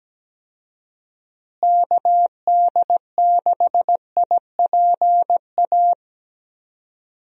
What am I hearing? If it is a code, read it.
Morse code, heard as a keyed tone, about 17 wpm, 700 Hz. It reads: KD6IPA
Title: USA Call Signs - 6 Characters - Repeated 2x - 17wpm